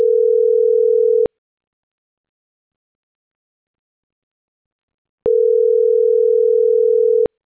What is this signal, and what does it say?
Ringback tone (North America), 440 + 480 Hz.